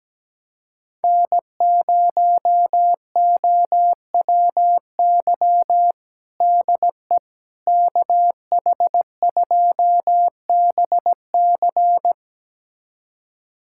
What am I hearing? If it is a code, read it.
Morse code, heard as a keyed tone, about 17 wpm, 700 Hz. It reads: N0OWY DE KH2BC
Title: USA Callsign DE Callsign - 5 Characters - 17wpm